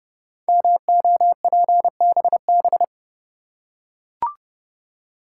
Morse code, heard as a keyed tone, about 30 wpm, 700 Hz. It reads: MOPBB
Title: Code Groups - 5 Letters - 30wpm